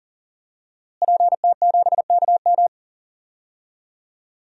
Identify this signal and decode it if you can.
Morse code, heard as a keyed tone, about 40 wpm, 700 Hz. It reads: PT7KM
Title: DXCC-100 Call Signs - 5 Characters - Repeated 2x - 40wpm